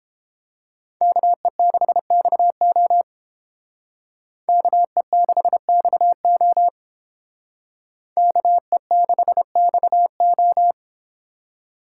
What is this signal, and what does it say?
Morse code, heard as a keyed tone, about 28 wpm, 700 Hz. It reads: KE6XO KE6XO KE6XO